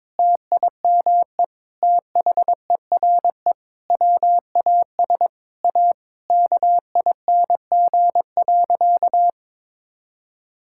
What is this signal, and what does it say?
Morse code, heard as a keyed tone, about 22 wpm, 700 Hz. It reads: TIME THERE WAS A KING.
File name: Sentences with 7-Word Phrases - 22wpm